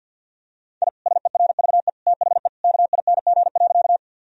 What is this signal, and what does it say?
Morse code, heard as a keyed tone, about 50 wpm, 700 Hz. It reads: I SERVE THE KING.
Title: Sentences from Top 1000 Words - Review Part 2 - 50wpm